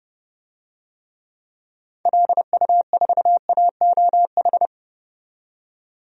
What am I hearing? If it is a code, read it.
Morse code, heard as a keyed tone, about 32 wpm, 700 Hz. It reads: LU4AOH
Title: DXCC-100 Call Signs - 6 Characters - 30wpm